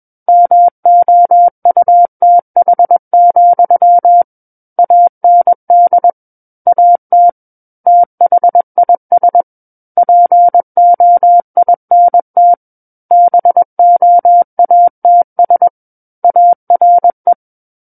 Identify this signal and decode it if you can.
Morse code, heard as a keyed tone, about 21 wpm, 700 Hz. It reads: MOUTH, AND AT THIS POINT BOATS ARE
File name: Book One / 12-Chapter 12 - 21 WPM